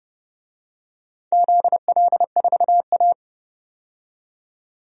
Morse code, heard as a keyed tone, about 32 wpm, 700 Hz. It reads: ZL4A
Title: DXCC-100 Call Signs - 4 Characters - 30wpm